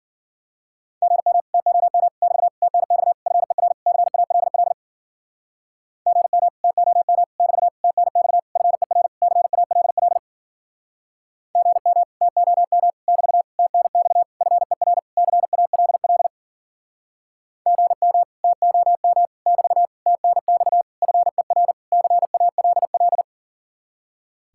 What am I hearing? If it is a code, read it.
Morse code, heard as a keyed tone, about 51 wpm, 700 Hz. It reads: GM TOM <BT> TNX FER CALL GM TOM <BT> TNX FER CALL GM TOM <BT> TNX FER CALL GM TOM <BT> TNX FER CALL